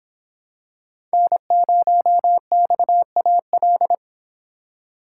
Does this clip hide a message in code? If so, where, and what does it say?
Yes: Morse code from 1.1 to 3.9 s, heard as a keyed tone, about 26 wpm, 700 Hz. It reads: N0XAL